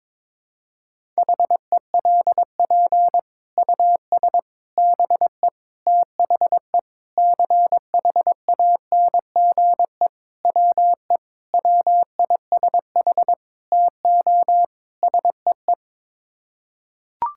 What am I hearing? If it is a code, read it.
Morse code, heard as a keyed tone, about 22 wpm, 700 Hz. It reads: HELP US BE THE CHANGE WE WISH TO SEE
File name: Sets of 9 Words - 22wpm